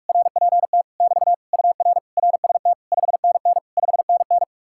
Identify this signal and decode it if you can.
Morse code, heard as a keyed tone, about 59 wpm, spaced as for 52 wpm, 700 Hz. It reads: RPT <BT> UR RST 5NN 5NN
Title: Speed-Racing Common QSO Phrases - 45wpm